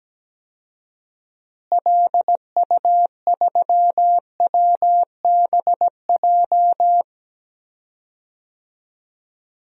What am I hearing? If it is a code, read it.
Morse code, heard as a keyed tone, about 17 wpm, 700 Hz. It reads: LU3WBJ
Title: DXCC-100 Call Signs - 6 Characters - Repeated 3x - 17wpm